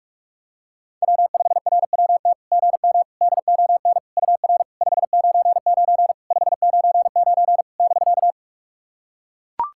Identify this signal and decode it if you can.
Morse code, heard as a keyed tone, about 44 wpm, 690 Hz. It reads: W5RWT GM DON UR 599 599 <BK>
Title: SOTA QSO Practice 45wpm